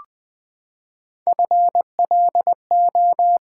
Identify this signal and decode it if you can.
Morse code, heard as a keyed tone, about 20 wpm, 700 Hz. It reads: FLO
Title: Build A Word - 3 Letters At A Time - 6-Letter Words - 20wpm